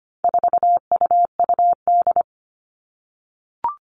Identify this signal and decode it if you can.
Morse code, heard as a keyed tone, about 25 wpm, 700 Hz. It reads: 4UUD
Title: All Cut Numbers - 4-Digit 25wpm